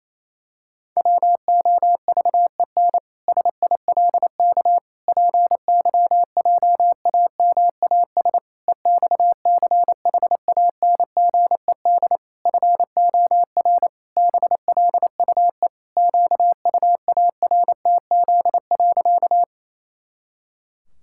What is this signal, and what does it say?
Morse code, heard as a keyed tone, about 28 wpm, 700 Hz. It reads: WOVEN SILK PYJAMAS EXCHANGED FOR BLUE QUARTZ.
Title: Pangram Challenge v2 - 28wpm